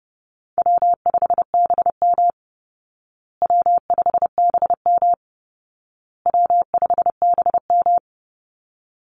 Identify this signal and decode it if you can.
Morse code, heard as a keyed tone, about 30 wpm, 700 Hz. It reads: W5BM W5BM W5BM
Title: Easy Call Signs Repeated 3x - 30wpm